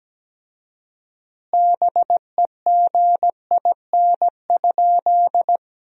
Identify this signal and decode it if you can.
Morse code, heard as a keyed tone, about 17 wpm, 700 Hz. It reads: BEGIN?
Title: 4x Word-Spacing - Sentences from Top 300 Words - 17wpm